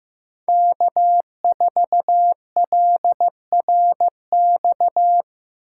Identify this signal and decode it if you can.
Morse code, heard as a keyed tone, about 15 wpm, 700 Hz. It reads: K4LRX
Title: K1USN SST QSO Practice - 15wpm